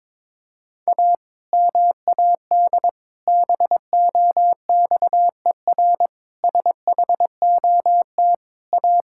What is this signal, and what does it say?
Morse code, heard as a keyed tone, about 22 wpm, 700 Hz. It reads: A MAD BOXER SHOT A
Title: Pangram Challenge v2 - 22wpm